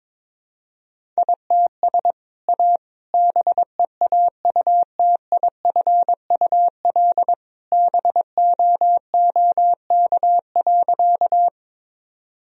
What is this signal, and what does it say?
Morse code, heard as a keyed tone, about 22 wpm, 700 Hz. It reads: ITS A BEAUTIFUL BOOK.
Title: Sentences from Top 1000 Words - Review Part 1 - 22wpm